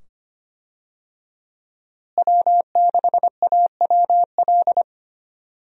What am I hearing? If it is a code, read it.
Morse code, heard as a keyed tone, about 25 wpm, 700 Hz. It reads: W6AWL
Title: USA Call Signs - 5 Characters - Repeated 2x - 25wpm